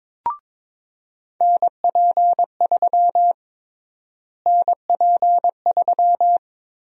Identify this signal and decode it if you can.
Morse code, heard as a keyed tone, about 21 wpm, 700 Hz. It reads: NP3 NP3
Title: USA Call Sign Prefixes - Repeated 2x - 22wpm